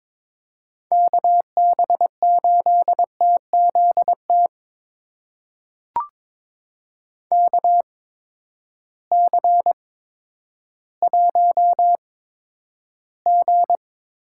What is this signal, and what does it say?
Morse code, heard as a keyed tone, about 21 wpm, spaced as for 11 wpm, 700 Hz. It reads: KB8TZT KC1G
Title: USA Call Signs - 6 Characters - 8x Character Spacing - 22wpm